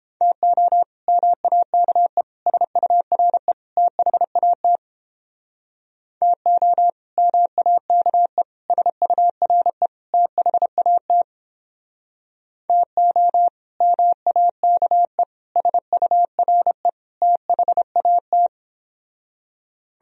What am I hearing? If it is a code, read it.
Morse code, heard as a keyed tone, about 29 wpm, 700 Hz. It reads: TO MAKE SURE THAT TO MAKE SURE THAT TO MAKE SURE THAT